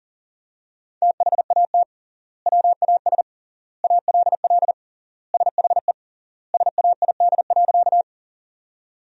Morse code, heard as a keyed tone, about 40 wpm, 700 Hz. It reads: THAT WAS ALL SHE SAID.
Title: 3x Word-Spacing - Sentences from Top 100 Words - 40wpm